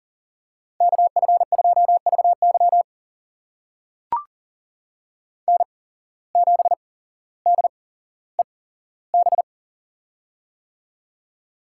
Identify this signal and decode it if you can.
Morse code, heard as a keyed tone, about 40 wpm, spaced as for 17 wpm, 700 Hz. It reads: KF2VY N7DEB
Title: USA Call Signs - 5 Characters - 8x Character Spacing - 40wpm